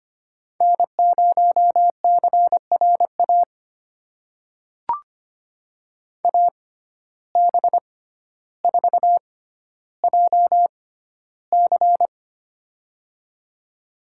Morse code, heard as a keyed tone, about 25 wpm, spaced as for 13 wpm, 700 Hz. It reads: N0CRA AB4JC
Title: USA Call Signs - 5 Characters - 6x Character Spacing - 25wpm